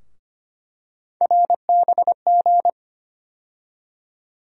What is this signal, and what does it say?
Morse code, heard as a keyed tone, about 25 wpm, 700 Hz. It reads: RBG
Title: USA Call Sign Postfixes - 3 Letters 25wpm